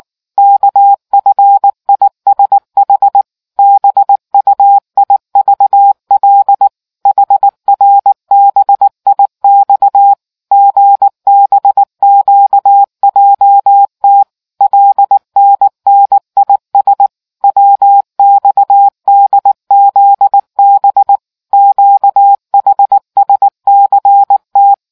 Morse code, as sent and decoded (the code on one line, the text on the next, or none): -.- ..-. .. ... .... / -... ..- .. ...- .-.. / .... .-. -... .. -..- / --. -... --.- .--- - / .-.. -. -. .. ... / .-- -..- -.. --.. -... / --.- .... ... -.-. -
KFISH BUIVL HRBIX GBQJT LNNIS WXDZB QHSCT